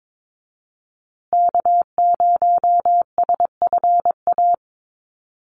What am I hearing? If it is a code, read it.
Morse code, heard as a keyed tone, about 22 wpm, 700 Hz. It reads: K0SFA
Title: USA Call Signs Repeated 2x 22wpm